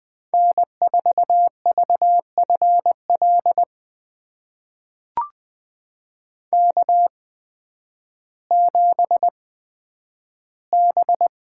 Morse code, heard as a keyed tone, about 20 wpm, spaced as for 11 wpm, 700 Hz. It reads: N4VFL K7B
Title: USA Call Signs - 5 Characters - 8x Character Spacing - 20wpm